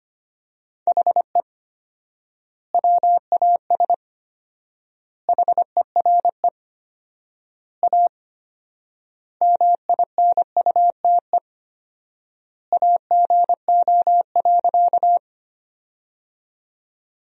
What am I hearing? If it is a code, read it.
Morse code, heard as a keyed tone, about 25 wpm, 700 Hz. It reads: HE WAS HERE A MINUTE AGO.
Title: 4x Word-Spacing - Sentences from Top 500 Words - 25wpm